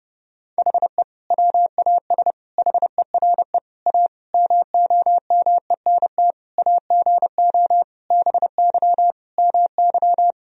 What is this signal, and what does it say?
Morse code, heard as a keyed tone, about 30 wpm, 700 Hz. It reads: HE WAS HERE A MOMENT AGO BY MY
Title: Sentences from Top 1000 Words - Review Part 3 - 30wpm